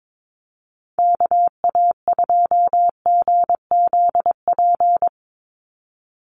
Morse code, heard as a keyed tone, about 22 wpm, 700 Hz. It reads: KA2GZP
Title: USA Call Signs 22wpm